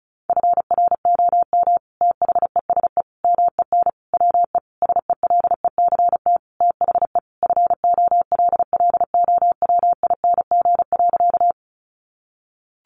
Morse code, heard as a keyed tone, about 34 wpm, 700 Hz. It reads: FROM THESE MEN WE SELECT THE FOLLOWING.
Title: Sentences from Top 1300 Words 35wpm